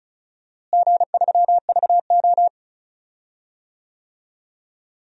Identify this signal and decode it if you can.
Morse code, heard as a keyed tone, about 36 wpm, 700 Hz. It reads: G3VO
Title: DXCC-100 Call Signs - 4 Characters - Repeated 3x - 35wpm